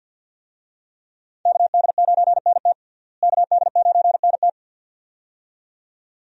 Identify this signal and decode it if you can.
Morse code, heard as a keyed tone, about 47 wpm, 700 Hz. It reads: KD9NT KD9NT
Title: USA Call Signs - 5 Characters - Repeated 2x - 50wpm